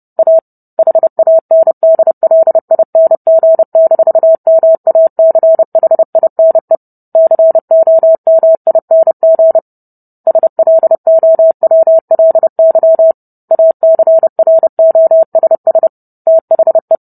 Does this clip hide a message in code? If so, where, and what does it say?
Yes: Morse code from 0.2 to 17.0 s, heard as a keyed tone, about 30 wpm, 650 Hz. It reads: A HANDLING-MACHINE COMING SLOWLY ACROSS THE